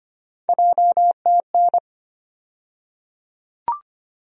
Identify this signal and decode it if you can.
Morse code, heard as a keyed tone, about 25 wpm, 700 Hz. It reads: JTN
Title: USA Call Sign Postfixes - 3 Letters 25wpm